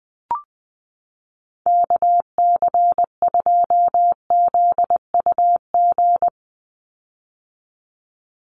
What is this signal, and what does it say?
Morse code, heard as a keyed tone, about 20 wpm, 700 Hz. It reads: KC2ZUG
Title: USA Call Signs 20wpm